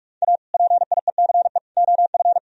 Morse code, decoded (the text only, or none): A PIECE OF